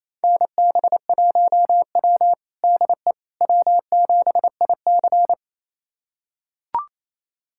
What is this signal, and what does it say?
Morse code, heard as a keyed tone, about 28 wpm, 700 Hz. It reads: NB1W DE W7IC
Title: USA Callsign DE Callsign - 4 Characters - 28wpm